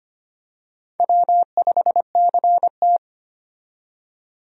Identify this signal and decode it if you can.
Morse code, heard as a keyed tone, about 25 wpm, 700 Hz. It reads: W5CT
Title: USA Call Signs - 4 Characters - 25wpm